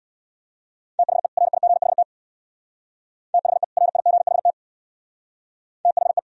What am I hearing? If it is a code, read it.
Morse code, heard as a keyed tone, about 60 wpm, 700 Hz. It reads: THE RIGHT THE RIGHT THE
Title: Speed-Racing - Sets of 2 Words 50wpm